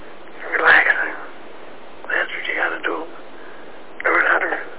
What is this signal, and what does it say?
Emergency Alert System (SAME) tones, heard as harsh buzzing bursts, with speech. It is heard from 0.4 to 4.4 s.